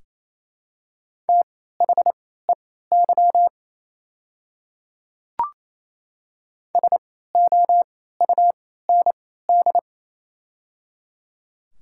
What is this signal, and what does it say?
Morse code, heard as a keyed tone, about 27 wpm, spaced as for 16 wpm, 700 Hz. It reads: THEY SOUND
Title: 3x Character Spacing - Top 100 Words - 28wpm